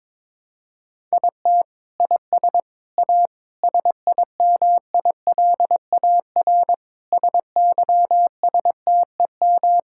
Morse code, heard as a keyed tone, about 22 wpm, 700 Hz. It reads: IT IS A SIMILAR SYSTEM
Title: Sentences from Top 1000 Words - Review Part 5 - 22wpm